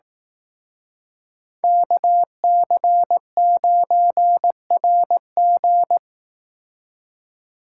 Morse code, heard as a keyed tone, about 18 wpm, 700 Hz. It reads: KC9RG
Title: Speed Racing - USA Call Signs 15wpm